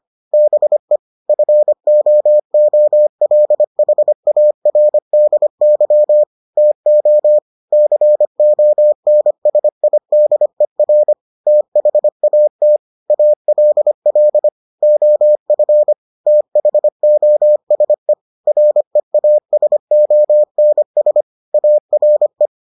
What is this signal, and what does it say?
Morse code, heard as a keyed tone, about 25 wpm, 600 Hz. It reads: BE FOOLHARDY TO CONSIDER THAT ALL OF THOSE REASONS ARE